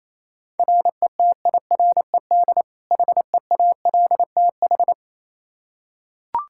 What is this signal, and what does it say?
Morse code, heard as a keyed tone, about 28 wpm, 700 Hz. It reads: RETIRED HEALTH
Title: Practice QSO - Exchange 3 - More Info - 28wpm